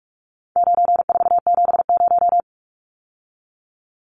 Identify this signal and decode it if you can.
Morse code, heard as a keyed tone, about 44 wpm, 700 Hz. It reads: 9470
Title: Numbers - 4-Digit 45wpm